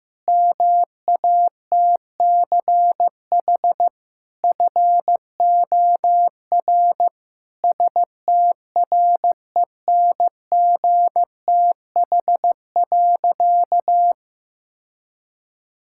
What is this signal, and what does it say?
Morse code, heard as a keyed tone, about 15 wpm, 700 Hz. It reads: MATCH FOR STRENGTH.
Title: Sentences from Top 1600 Words 15wpm